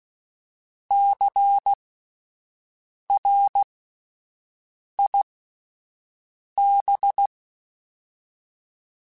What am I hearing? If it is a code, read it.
Morse code, heard as a keyed tone, about 16 wpm, spaced as for 5 wpm, 800 Hz. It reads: CRIB
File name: Spaced Alphas / Spaced Alphas mp3s / Cs sp @10